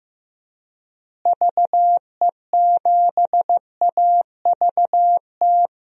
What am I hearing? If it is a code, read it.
Morse code, heard as a keyed tone, about 15 wpm, 700 Hz. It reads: VE7AVT
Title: DXCC-100 Call Signs - 6 Characters - Repeated 3x - 15wpm